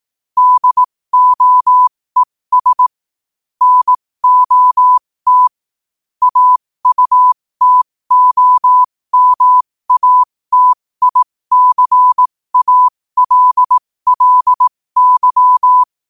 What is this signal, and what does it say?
Morse code, heard as a keyed tone, about 18 wpm, spaced as for 15 wpm, 1000 Hz. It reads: DOES NOT AUTOMATICALLY
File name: qotdfiles / 15wpm